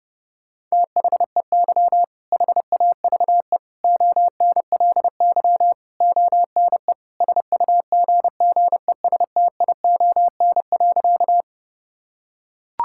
Morse code, heard as a keyed tone, about 30 wpm, 700 Hz. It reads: THEY HAVE ONLY ONE SUGGESTION.
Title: Sentences from Top 1600 Words 30wpm